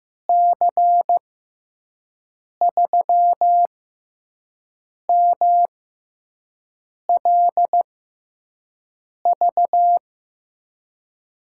Morse code, heard as a keyed tone, about 15 wpm, spaced as for 5 wpm, 700 Hz. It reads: C3MLV
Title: USA Call Signs - 6 Characters - 6x Character Spacing - 15wpm